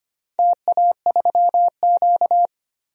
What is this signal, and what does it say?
Morse code, heard as a keyed tone, about 25 wpm, 700 Hz. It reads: TA3Q
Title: DXCC-100 Call Signs - 4 Characters - Repeated 3x - 25wpm